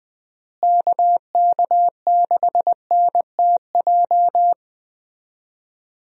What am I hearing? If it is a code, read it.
Morse code, heard as a keyed tone, about 20 wpm, 700 Hz. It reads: KK6NTJ
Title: USA Call Signs - 6 Characters - Repeated 3x - 20wpm